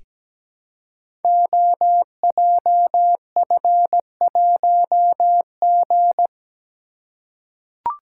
Morse code, heard as a keyed tone, about 17 wpm, 700 Hz. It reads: OJF1G
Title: Code Groups - 5 Letter:Numbers - 17wpm